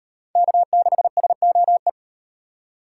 Morse code, heard as a keyed tone, about 38 wpm, 700 Hz. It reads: K6SOE